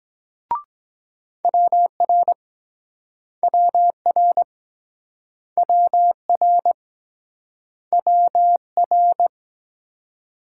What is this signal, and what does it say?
Morse code, heard as a keyed tone, about 23 wpm, 700 Hz. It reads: WR WR WR WR